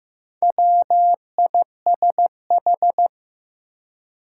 Morse code, heard as a keyed tone, about 15 wpm, 700 Hz. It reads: WISH
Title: Four Letter Words - 15wpm